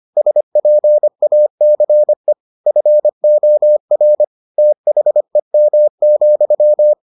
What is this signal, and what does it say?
Morse code, heard as a keyed tone, about 25 wpm, 600 Hz. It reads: SPACE FOR THEM,